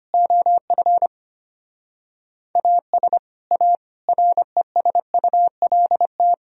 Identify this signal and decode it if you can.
Morse code, heard as a keyed tone, about 25 wpm, 700 Hz. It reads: OF AS A RESULT
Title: Speed-Racing - Sets of 4 Words 25wpm